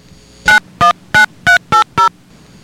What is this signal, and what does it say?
DTMF, heard as telephone touch tones, tones of about 120 ms. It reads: 919380